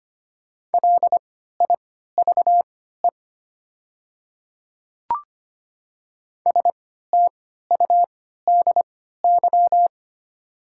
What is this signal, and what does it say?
Morse code, heard as a keyed tone, about 25 wpm, spaced as for 14 wpm, 700 Hz. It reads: LIVE STUDY
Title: 3x Character Spacing - Top 500 Words - 25wpm